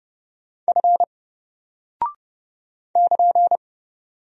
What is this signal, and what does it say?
Morse code, heard as a keyed tone, about 30 wpm, 700 Hz. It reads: F <KN>
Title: Single Character (Letter:Number:Punctuation:Prosign) Alt 30wpm